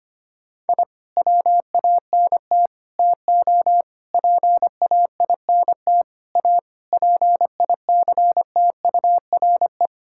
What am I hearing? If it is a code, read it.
Morse code, heard as a keyed tone, about 25 wpm, 700 Hz. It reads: I WANT TO PAINT A PICTURE
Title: Sentences from Top 1000 Words - Review Part 6 - 25wpm